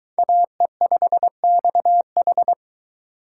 Morse code, heard as a keyed tone, about 21 wpm, 700 Hz. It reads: AE5XH